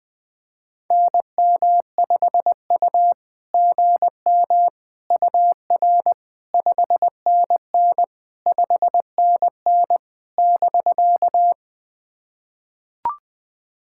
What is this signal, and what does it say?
Morse code, heard as a keyed tone, about 20 wpm, 690 Hz. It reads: NM5U GM UR 5NN 5NN <BK>
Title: SOTA QSO Practice 20wpm